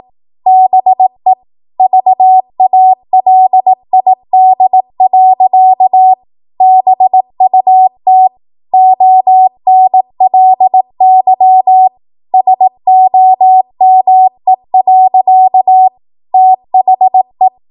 Morse code, heard as a keyed tone, about 18 wpm, 750 Hz. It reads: BE VALID. BUT ONLY SOME. THE
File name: Morse / 18WPM